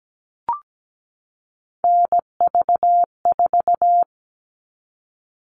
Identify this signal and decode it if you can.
Morse code, heard as a keyed tone, about 17 wpm, 700 Hz. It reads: NV4
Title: All Cut Numbers - 3-Digit 17wpm